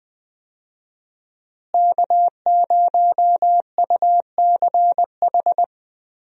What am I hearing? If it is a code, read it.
Morse code, heard as a keyed tone, about 20 wpm, 700 Hz. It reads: K0UCH
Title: USA Call Signs - 5 Characters - Repeated 3x - 20wpm